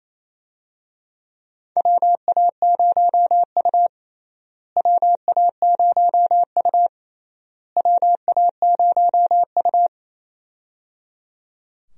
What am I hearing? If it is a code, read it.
Morse code, heard as a keyed tone, about 28 wpm, 700 Hz. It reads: WA0U WA0U WA0U
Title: USA Call Signs - 4 Characters - Repeated 3x - 28wpm